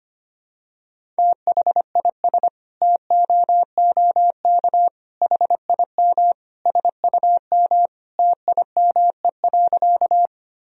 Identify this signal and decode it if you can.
Morse code, heard as a keyed tone, about 25 wpm, 700 Hz. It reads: THIS TOOK HIM SUM TIME.
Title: Sentences from Top 300 Words and Abbreviations 25wpm